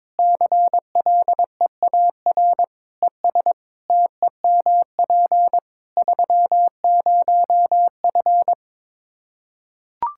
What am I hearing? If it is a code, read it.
Morse code, heard as a keyed tone, about 22 wpm, 700 Hz. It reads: CLEAR ES TEMP 30F
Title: Practice QSO - Exchange 2 - Info - 22wpm